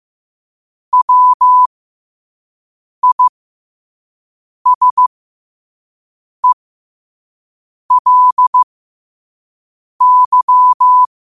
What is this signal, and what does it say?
Morse code, heard as a keyed tone, about 15 wpm, spaced as for 5 wpm, 1000 Hz. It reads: WISELY
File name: 5wpm